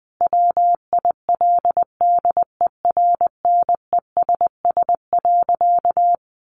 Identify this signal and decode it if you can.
Morse code, heard as a keyed tone, about 20 wpm, 700 Hz. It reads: WILDERNESS.
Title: A Sound Of Thunder 20wpm